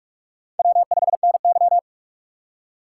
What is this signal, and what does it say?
Morse code, heard as a keyed tone, about 44 wpm, 700 Hz. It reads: W5NY